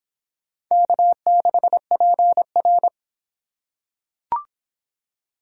Morse code, heard as a keyed tone, about 25 wpm, 700 Hz. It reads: K6PR